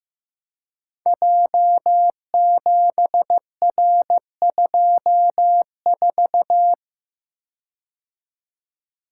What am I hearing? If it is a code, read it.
Morse code, heard as a keyed tone, about 15 wpm, 700 Hz. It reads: J7R24
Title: Code Groups - 5 Letter:Numbers - 15wpm